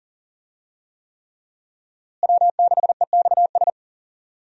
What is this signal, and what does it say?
Morse code, heard as a keyed tone, about 39 wpm, 700 Hz. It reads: W6EXS
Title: USA Call Signs - 5 Characters - Repeated 3x - 40wpm